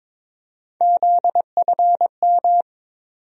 Morse code, heard as a keyed tone, about 21 wpm, 700 Hz. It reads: ZFM
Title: USA Call Sign Postfixes - 3 Letters 22wpm